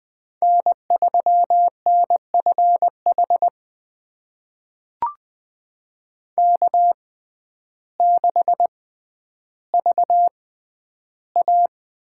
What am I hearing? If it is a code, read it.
Morse code, heard as a keyed tone, about 20 wpm, spaced as for 11 wpm, 700 Hz. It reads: N3NFH K6VA
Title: USA Call Signs - 5 Characters - 6x Character Spacing - 20wpm